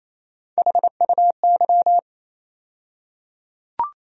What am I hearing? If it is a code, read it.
Morse code, heard as a keyed tone, about 28 wpm, 700 Hz. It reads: HUY
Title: USA Call Sign Postfixes - 3 Letters 28wpm